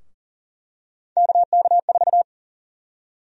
Morse code, heard as a keyed tone, about 40 wpm, 700 Hz. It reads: KK4
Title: USA Call Sign Prefixes - Repeated 2x - 40wpm